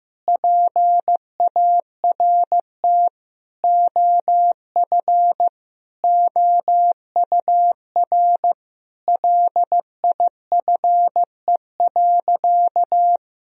Morse code, heard as a keyed tone, about 15 wpm, 700 Hz. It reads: PART OF OUR LIFE.
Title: Sentences from Top 1000 Words - Review Part 1 - 15wpm